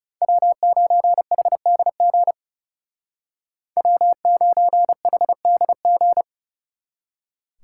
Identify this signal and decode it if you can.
Morse code, heard as a keyed tone, about 38 wpm, 700 Hz. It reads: W9HDG W9HDG